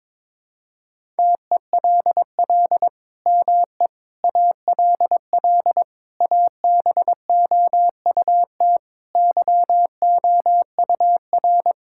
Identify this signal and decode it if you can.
Morse code, heard as a keyed tone, about 22 wpm, 700 Hz. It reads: TELL ME ALL ABOUT YOUR
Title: Sentences from Top 1000 Words - Review Part 6 - 22wpm